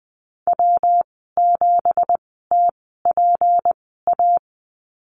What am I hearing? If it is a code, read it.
Morse code, heard as a keyed tone, about 20 wpm, spaced as for 14 wpm, 700 Hz. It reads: W7TPA
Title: 231 Lesson 37g - Letter x - Call Signs 20wpm